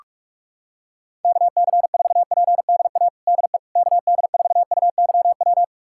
Morse code, heard as a keyed tone, about 41 wpm, 700 Hz. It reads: KC4PDA DE KD4UYW
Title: USA Callsign DE Callsign - 6 Characters - 45wpm